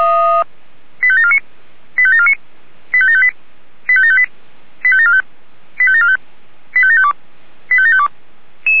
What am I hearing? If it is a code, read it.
Five-tone selective call (ZVEI), heard as quick runs of five tones, tones of about 70 ms. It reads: 85749 85749 85758 85758 85745 85745 85752 85752